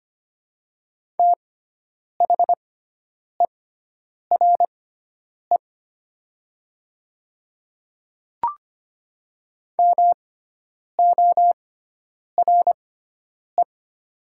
Morse code, heard as a keyed tone, about 25 wpm, spaced as for 9 wpm, 700 Hz. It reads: THERE MORE
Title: 6x Character Spacing - Top 100 Words - 25wpm